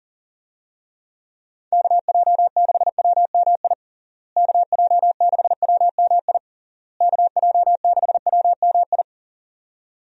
Morse code, heard as a keyed tone, about 37 wpm, 700 Hz. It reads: KJ6WMI KJ6WMI KJ6WMI
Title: USA Call Signs - 6 Characters - Repeated 3x - 40wpm